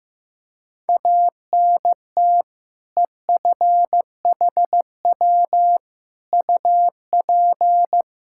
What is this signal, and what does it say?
Morse code, heard as a keyed tone, about 16 wpm, 700 Hz. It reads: ANT EFHW UP
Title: Practice QSO - Exchange 2 - Info - 15wpm